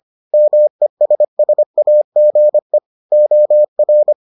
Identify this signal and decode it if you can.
Morse code, heard as a keyed tone, about 25 wpm, 600 Hz. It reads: MESSAGE OR